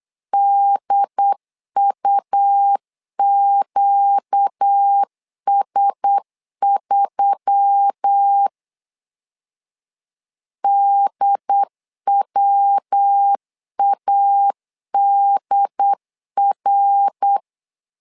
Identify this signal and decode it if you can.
Morse code, heard as a keyed tone, about 8 wpm, 800 Hz. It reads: DUQS3 DWADR
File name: LEZIONI / 4 Q1S3 8wpm 3037